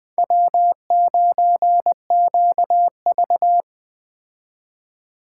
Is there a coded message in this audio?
Yes: Morse code from 0.2 to 3.6 s, heard as a keyed tone, about 20 wpm, 700 Hz. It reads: W9QV